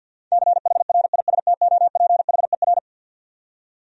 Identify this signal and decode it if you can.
Morse code, heard as a keyed tone, about 49 wpm, 700 Hz. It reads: CHRISTOPHER
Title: USA Common Names - 50wpm